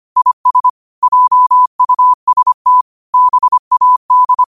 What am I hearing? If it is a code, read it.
Morse code, heard as a keyed tone, about 25 wpm, 1000 Hz. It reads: IS JUST BAD
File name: qotdfiles / 25wpm